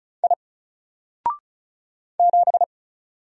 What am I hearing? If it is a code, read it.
Morse code, heard as a keyed tone, about 34 wpm, 700 Hz. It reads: I 7
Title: Single Letter-Number Alt 35wpm